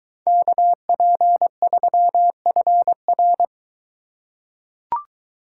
Morse code, heard as a keyed tone, about 23 wpm, 700 Hz. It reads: KP3FR